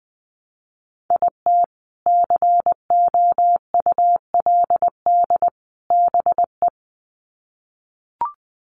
Morse code, heard as a keyed tone, about 20 wpm, 700 Hz. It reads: IT COULD BE
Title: Sets of 3 Words 20wpm